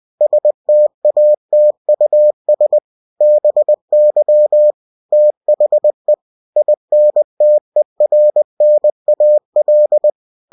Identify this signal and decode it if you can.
Morse code, heard as a keyed tone, about 20 wpm, 600 Hz. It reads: STATUS BY THE INTERNAL